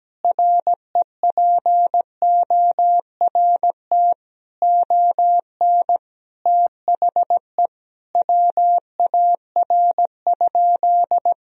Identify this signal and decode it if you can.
Morse code, heard as a keyed tone, about 17 wpm, 700 Hz. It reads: REPORT ON THE WAR?
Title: Sentences from Top 1000 Words - Review Part 4 - 17wpm